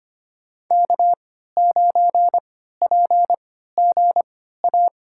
Morse code, heard as a keyed tone, about 25 wpm, spaced as for 14 wpm, 700 Hz. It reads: K9PGA
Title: USA Call Signs - 3x Character Spacing - 25wpm